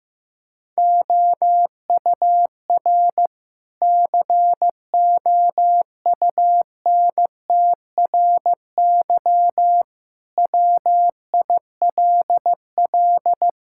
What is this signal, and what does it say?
Morse code, heard as a keyed tone, about 15 wpm, 700 Hz. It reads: OUR COUNTRY WILL
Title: Sentences from Top 1000 Words - Review Part 5 - 15wpm